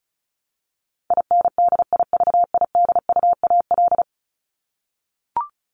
Morse code, heard as a keyed tone, about 34 wpm, 700 Hz. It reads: INDIVIDUAL
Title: Top 501-750 Words 35wpm